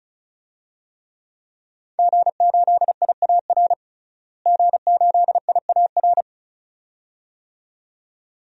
Morse code, heard as a keyed tone, about 35 wpm, 700 Hz. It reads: G8IAR G8IAR
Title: DXCC-100 Call Signs - 5 Characters - Repeated 2x - 35wpm